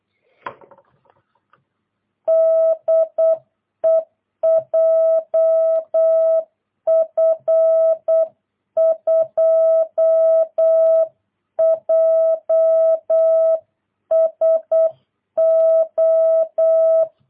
Morse code, heard as a keyed tone, about 8 wpm, 650 Hz. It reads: DE JF2JSO
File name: jh8cbh / 2022cw